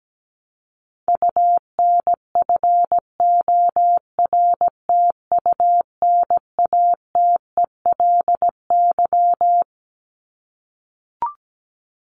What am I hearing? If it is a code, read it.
Morse code, heard as a keyed tone, about 17 wpm, 700 Hz. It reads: UNFORTUNATELY
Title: LY Suffix - Top 100 Words with Base Words 17wpm